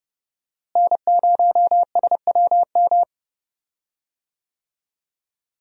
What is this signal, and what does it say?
Morse code, heard as a keyed tone, about 31 wpm, 700 Hz. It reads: N0SWM
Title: USA Call Signs - 5 Characters - Repeated 2x - 30wpm